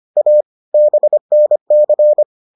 Morse code, heard as a keyed tone, about 25 wpm, 600 Hz. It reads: A BNC